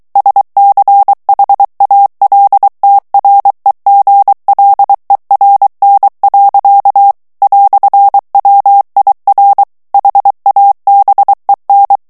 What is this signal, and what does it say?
Morse code, heard as a keyed tone, about 23 wpm, 800 Hz. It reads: SCHALTREGLERN. "WIR HABEN
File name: morse mp3 / ALLEMAND texte clair 1 vitesse 22 mots minute